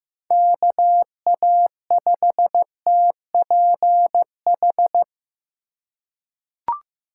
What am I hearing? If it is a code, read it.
Morse code, heard as a keyed tone, about 15 wpm, 700 Hz. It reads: KA5TPH
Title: USA Call Signs - 6 Characters - 2x Character Spacing - 15wpm